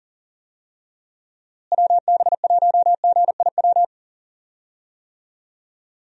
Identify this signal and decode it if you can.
Morse code, heard as a keyed tone, about 40 wpm, 700 Hz. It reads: WB1GIW
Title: USA Call Signs - 6 Characters - Repeated 3x - 40wpm